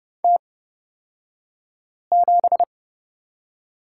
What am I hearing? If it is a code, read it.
Morse code, heard as a keyed tone, about 30 wpm, spaced as for 5 wpm, 700 Hz. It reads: T7
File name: Lesson 36d - BT - Two Characters 30wpm